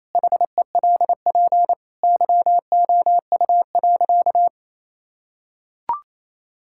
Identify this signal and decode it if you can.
Morse code, heard as a keyed tone, about 28 wpm, 700 Hz. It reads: HELP YOU.
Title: Sentences from Top 1000 Words - Review Part 6 - 28wpm